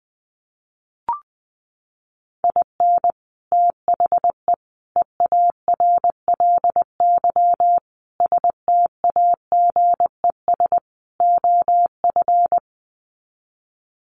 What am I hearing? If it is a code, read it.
Morse code, heard as a keyed tone, about 20 wpm, 700 Hz. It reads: IN THE EARLY STAGES OF
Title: Sets of 5 Words Encore 20wpm